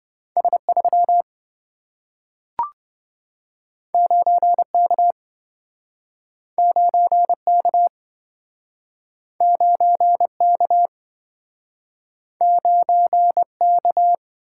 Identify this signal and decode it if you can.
Morse code, heard as a keyed tone, about 27 wpm, 700 Hz. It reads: S3 9K 9K 9K 9K